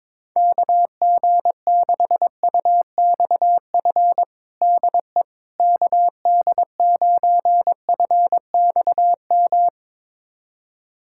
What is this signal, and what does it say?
Morse code, heard as a keyed tone, about 22 wpm, 700 Hz. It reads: KG6UXF DE KD9FXM
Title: USA Callsign DE Callsign 22wpm